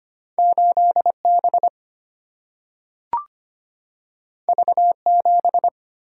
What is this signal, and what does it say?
Morse code, heard as a keyed tone, about 25 wpm, 700 Hz. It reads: 8 B V 7
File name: Two Letter-Number Combinations - 25wpm